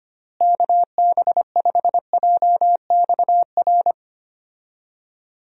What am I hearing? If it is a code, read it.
Morse code, heard as a keyed tone, about 25 wpm, 700 Hz. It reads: KB5JXR
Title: USA Call Signs - 6 Characters - Repeated 3x - 25wpm